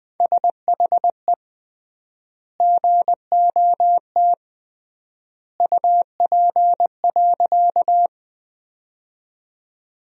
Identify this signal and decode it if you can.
Morse code, heard as a keyed tone, about 20 wpm, 700 Hz. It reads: SHE GOT UP.
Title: 3x Word-Spacing - Sentences from Top 300 Words - 20wpm